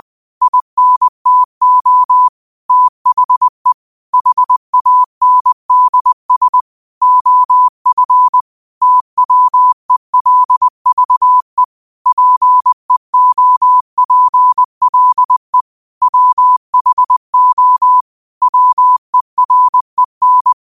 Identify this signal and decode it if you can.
Morse code, heard as a keyed tone, about 20 wpm, 1000 Hz. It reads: INTO THE HANDS OF TWELVE PEOPLE WHO WEREN
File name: qotdfiles / 20wpm